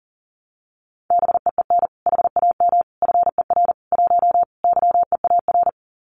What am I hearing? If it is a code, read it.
Morse code, heard as a keyed tone, about 40 wpm, 700 Hz. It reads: BEEN HAM FER 1 YEAR
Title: Common QSO Phrases Encore 40wpm